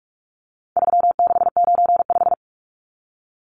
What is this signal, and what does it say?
Morse code, heard as a keyed tone, about 44 wpm, 700 Hz. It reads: 3695
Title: Numbers - 4-Digit 45wpm